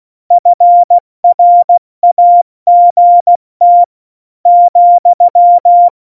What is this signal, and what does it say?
Morse code, heard as a keyed tone, about 16 wpm, 700 Hz. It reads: FRAGT ,
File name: Morsekurs / Witze / Witz 14. Januar 2020, 70 F80